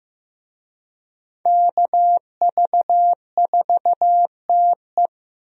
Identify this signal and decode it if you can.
Morse code, heard as a keyed tone, about 15 wpm, 700 Hz. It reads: KV4TE
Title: USA Call Signs - 5 Characters - Repeated 2x - 15wpm